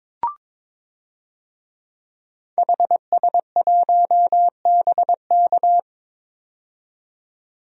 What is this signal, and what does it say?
Morse code, heard as a keyed tone, about 21 wpm, 700 Hz. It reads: HS1BK
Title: DXCC-100 Call Signs - 5 Characters - 22wpm